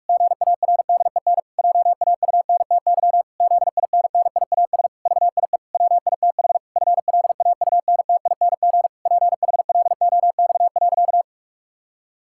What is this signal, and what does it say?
Morse code, heard as a keyed tone, about 46 wpm, 700 Hz. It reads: GARDEN JAUNTY ZINNIAS VIE WITH FLAUNTING PHLOX.
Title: Pangram Challenge v2 - 45wpm